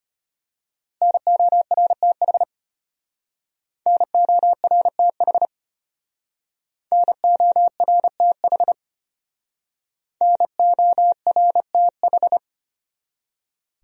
Morse code, heard as a keyed tone, about 32 wpm, 700 Hz. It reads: NORTH NORTH NORTH NORTH